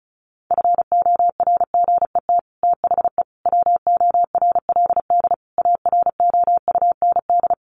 Morse code, heard as a keyed tone, about 34 wpm, 700 Hz. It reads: FORGET THE WORLD AROUND
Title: Sentences from Top 1200 Words 35wpm